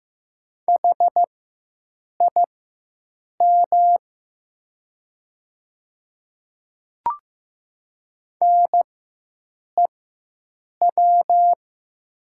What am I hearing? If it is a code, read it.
Morse code, heard as a keyed tone, about 15 wpm, spaced as for 7 wpm, 700 Hz. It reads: HIM NEW
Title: 4x Character Spacing - Top 200 Words - 15wpm